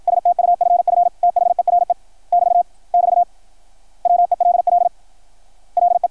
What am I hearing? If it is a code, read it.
Morse code, heard as a keyed tone, about 53 wpm, 690 Hz. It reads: STUFF THERE <BT> <BT> WELL LE
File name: archived / CRSnet